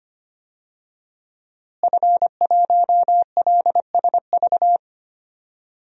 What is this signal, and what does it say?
Morse code, heard as a keyed tone, about 25 wpm, 700 Hz. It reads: F1LSV
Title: DXCC-100 Call Signs - 5 Characters - Repeated 2x - 25wpm